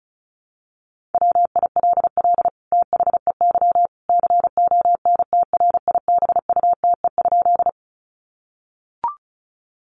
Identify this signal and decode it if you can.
Morse code, heard as a keyed tone, about 34 wpm, 700 Hz. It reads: WILL THEY CONTRIBUTE?
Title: Sentences from Top 1400 Words 35wpm